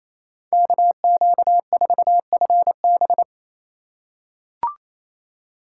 Morse code, heard as a keyed tone, about 28 wpm, 700 Hz. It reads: KQ4FB
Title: USA Call Signs - 5 Characters - 8x Character Spacing - 28wpm